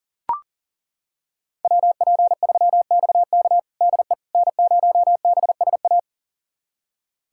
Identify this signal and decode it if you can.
Morse code, heard as a keyed tone, about 40 wpm, 700 Hz. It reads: WP3XK DE N0BSA
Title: USA Callsign DE Callsign - 5 Characters - 40wpm